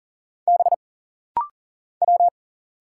Morse code, heard as a keyed tone, about 40 wpm, 700 Hz. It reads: B W
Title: Single Letters - Rapid-Fire 40wpm